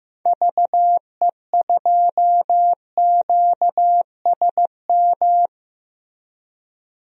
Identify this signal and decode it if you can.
Morse code, heard as a keyed tone, about 16 wpm, 700 Hz. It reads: VE2QSM
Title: DXCC-100 Call Signs - 6 Characters - Repeated 2x - 15wpm